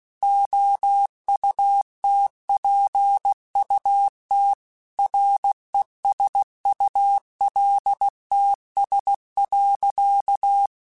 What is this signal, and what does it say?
Morse code, heard as a keyed tone, about 16 wpm, 800 Hz. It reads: OUTPUT RESULTS.
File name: Technical Subjects / Technical Subjects-mp3s / Gates Intro @15